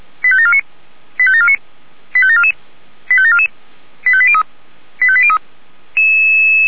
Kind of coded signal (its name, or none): five-tone selective call